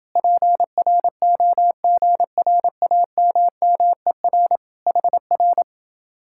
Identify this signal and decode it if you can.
Morse code, heard as a keyed tone, about 28 wpm, 700 Hz. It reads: PROGRAMMER HR